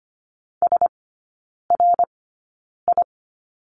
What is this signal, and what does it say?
Morse code, heard as a keyed tone, about 25 wpm, spaced as for 9 wpm, 700 Hz. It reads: SRI
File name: Lesson 17e - Letter u - Three Characters 25wpm